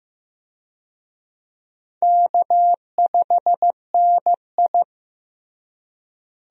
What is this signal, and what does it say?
Morse code, heard as a keyed tone, about 15 wpm, 700 Hz. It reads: K5NI
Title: USA Call Signs - 4 Characters - Repeated 3x - 15wpm